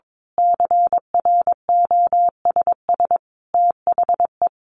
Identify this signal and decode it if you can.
Morse code, heard as a keyed tone, about 22 wpm, 700 Hz. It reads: CROSS THE
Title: A Sound Of Thunder 22wpm